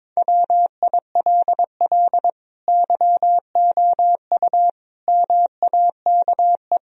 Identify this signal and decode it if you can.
Morse code, heard as a keyed tone, about 22 wpm, 700 Hz. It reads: WILL YOU MAKE
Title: Sentences from Top 1000 Words - Review Part 3 - 22wpm